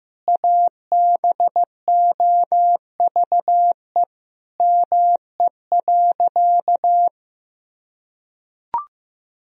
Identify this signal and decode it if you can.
Morse code, heard as a keyed tone, about 15 wpm, 700 Hz. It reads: ABOVE ME.
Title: Sentences from Top 1000 Words - Review Part 2 - 15wpm